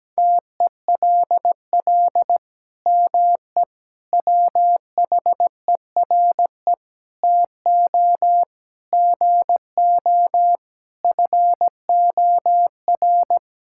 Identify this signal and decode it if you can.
Morse code, heard as a keyed tone, about 17 wpm, 700 Hz. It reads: TELL ME WHERE TO GO FOR
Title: Sentences from Top 1600 Words 17wpm